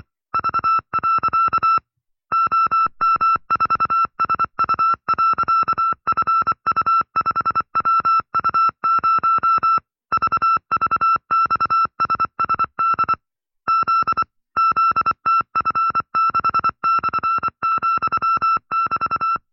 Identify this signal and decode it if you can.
Morse code, heard as a keyed tone, about 25 wpm, 1340 Hz. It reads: V. OM4SU.FU5WU0 VVXSSD Z ZTF6/,<BT>